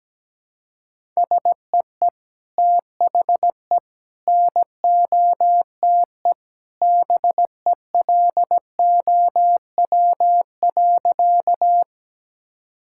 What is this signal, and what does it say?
Morse code, heard as a keyed tone, about 17 wpm, 700 Hz. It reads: SEE THE NOTE BELOW.
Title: Sentences from Top 1000 Words - Review Part 5 - 17wpm